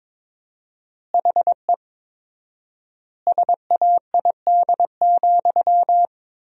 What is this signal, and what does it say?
Morse code, heard as a keyed tone, about 22 wpm, 700 Hz. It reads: HE SAID,
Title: 4x Word-Spacing - Sentences from Top 300 Words - 22wpm